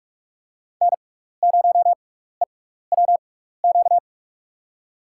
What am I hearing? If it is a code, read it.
Morse code, heard as a keyed tone, about 44 wpm, spaced as for 15 wpm, 700 Hz. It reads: N0EWQ
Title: USA Call Signs - 5 Characters - 6x Character Spacing - 45wpm